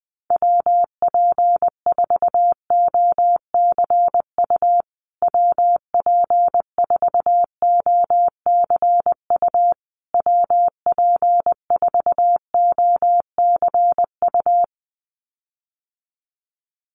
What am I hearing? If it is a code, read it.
Morse code, heard as a keyed tone, about 20 wpm, 700 Hz. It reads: WP4OCU WP4OCU WP4OCU
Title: USA Call Signs Repeated 3x 20wpm